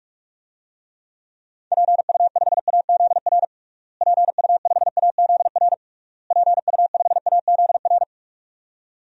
Morse code, heard as a keyed tone, about 42 wpm, 700 Hz. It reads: PU5AZR PU5AZR PU5AZR
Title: DXCC-100 Call Signs - 6 Characters - Repeated 3x - 45wpm